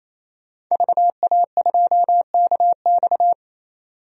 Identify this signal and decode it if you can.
Morse code, heard as a keyed tone, about 28 wpm, 700 Hz. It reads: VA2KX
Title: DXCC-100 Call Signs - 5 Characters - 28wpm